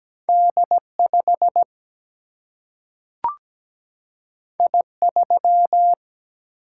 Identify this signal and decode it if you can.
Morse code, heard as a keyed tone, about 17 wpm, 700 Hz. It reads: D 5 I 3
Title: Single-Letter Single-Number 17wpm